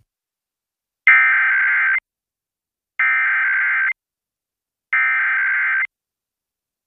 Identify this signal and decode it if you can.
Emergency Alert System (SAME) tones (header bursts), heard as harsh buzzing bursts. It is heard from 1.1 to 5.8 s.